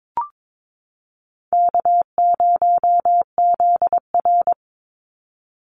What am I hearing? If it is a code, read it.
Morse code, heard as a keyed tone, about 22 wpm, 700 Hz. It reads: K0ZR
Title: Easy Call Signs Repeated 3x - 22wpm